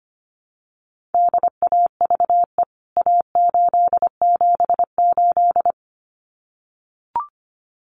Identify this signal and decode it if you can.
Morse code, heard as a keyed tone, about 25 wpm, 700 Hz. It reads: DAVE A878
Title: CWT Exchange 25wpm